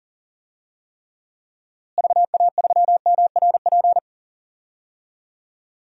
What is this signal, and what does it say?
Morse code, heard as a keyed tone, about 40 wpm, 700 Hz. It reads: VA3MRP
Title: DXCC-100 Call Signs - 6 Characters - Repeated 2x - 40wpm